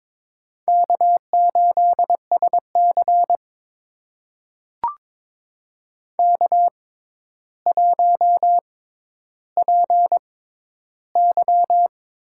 Morse code, heard as a keyed tone, about 22 wpm, spaced as for 11 wpm, 700 Hz. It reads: K8SC K1PY
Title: USA Call Signs - 4 Characters - 6x Character Spacing - 22wpm